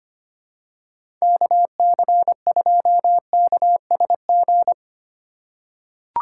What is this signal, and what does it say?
Morse code, heard as a keyed tone, about 25 wpm, 700 Hz. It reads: KC2KSG
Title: USA Call Signs - 2x Character Spacing - 25wpm